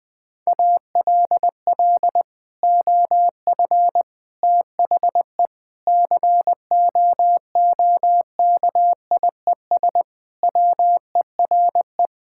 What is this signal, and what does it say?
Morse code, heard as a keyed tone, about 20 wpm, 700 Hz. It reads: ALL OF THE COOKIES WERE
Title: Sentences with 2-Word Phrases - Spoken Beginning - 20wpm